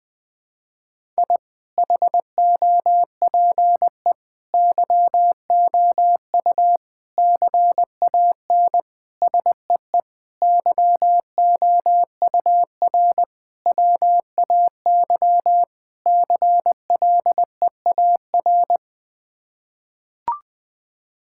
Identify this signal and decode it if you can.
Morse code, heard as a keyed tone, about 20 wpm, 700 Hz. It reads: I HOPE YOU CAN SEE YOUR WAY CLEAR
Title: Sets of 8 Words - 20wpm